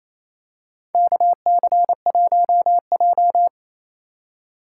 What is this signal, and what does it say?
Morse code, heard as a keyed tone, about 27 wpm, 700 Hz. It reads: KC1J
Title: USA Call Signs - 4 Characters - 4x Character Spacing - 28wpm